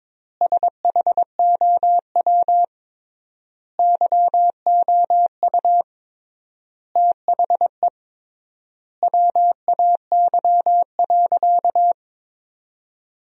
Morse code, heard as a keyed tone, about 22 wpm, 700 Hz. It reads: SHOW YOU THE WAY.
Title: 3x Word-Spacing - Sentences from Top 300 Words - 22wpm